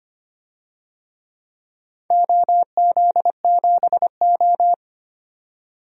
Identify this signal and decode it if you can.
Morse code, heard as a keyed tone, about 25 wpm, 700 Hz. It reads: OZ7O
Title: DXCC-100 Call Signs - 4 Characters - 25wpm